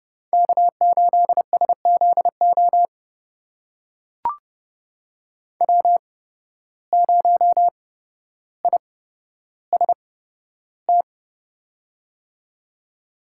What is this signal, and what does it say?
Morse code, heard as a keyed tone, about 32 wpm, spaced as for 13 wpm, 700 Hz. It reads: K8SZO W0IST
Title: USA Call Signs - 5 Characters - 8x Character Spacing - 30wpm